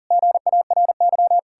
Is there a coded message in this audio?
Yes: Morse code from 0.1 to 1.4 s, heard as a keyed tone, about 52 wpm, 700 Hz. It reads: GARY